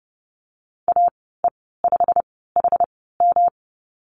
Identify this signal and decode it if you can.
Morse code, heard as a keyed tone, about 30 wpm, spaced as for 17 wpm, 700 Hz. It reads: AE5HM
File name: course / 30:10wpm / 116 Lesson 19g - Letter w - Call Signs 30wpm